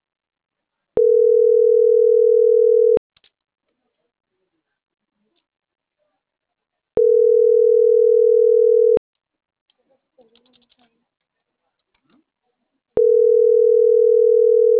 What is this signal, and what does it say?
Ringback tone (North America), 440 + 480 Hz.